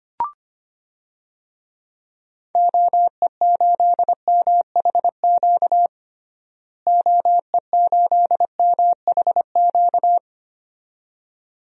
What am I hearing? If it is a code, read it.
Morse code, heard as a keyed tone, about 25 wpm, 700 Hz. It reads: OE8MHQ OE8MHQ
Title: DXCC-100 Call Signs - 6 Characters - Repeated 2x - 25wpm